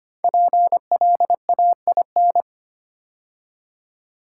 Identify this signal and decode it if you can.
Morse code, heard as a keyed tone, about 25 wpm, 700 Hz. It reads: PLAIN
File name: 5 Letter Words - First 3 Letters Given - 25wpm